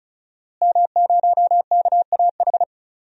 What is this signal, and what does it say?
Morse code, heard as a keyed tone, about 36 wpm, 700 Hz. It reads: M0KAH
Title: DXCC-100 Call Signs - 5 Characters - Repeated 3x - 35wpm